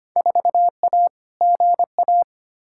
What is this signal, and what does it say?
Morse code, heard as a keyed tone, about 25 wpm, 700 Hz. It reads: 4A GA
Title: Field Day Exchanges - 25wpm